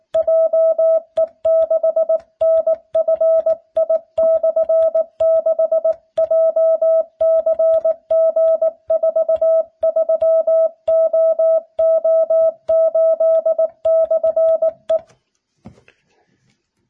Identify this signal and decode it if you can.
Morse code, heard as a keyed tone, about 19 wpm, 650 Hz. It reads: JE6NFI/6 JCG43OO8/E